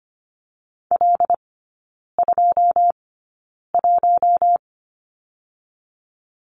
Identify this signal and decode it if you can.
Morse code, heard as a keyed tone, about 25 wpm, spaced as for 9 wpm, 700 Hz. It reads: L21
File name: course / 25:10wpm / 093 Lesson 16e - Letter c - Three Characters 25wpm